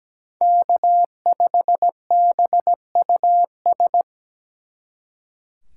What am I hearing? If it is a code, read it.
Morse code, heard as a keyed tone, about 20 wpm, 700 Hz. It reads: K5BUS
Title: Speed Racing - USA 1x3 Call Signs 17wpm